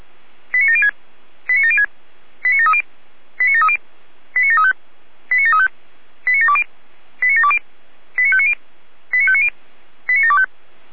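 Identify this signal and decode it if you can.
Five-tone selective call (ZVEI), heard as quick runs of five tones, tones of about 70 ms. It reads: 89706 89706 89730 89730 89735 89735 89720 89720 89590 89590 89725
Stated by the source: ZVEI five-tone selective call 89752 received on SDR.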